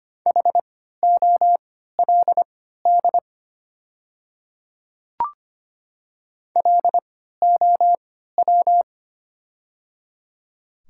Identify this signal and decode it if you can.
Morse code, heard as a keyed tone, about 25 wpm, spaced as for 14 wpm, 690 Hz. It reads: HOLD LOW
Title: 3x Character Spacing - Top 500 Words - 25wpm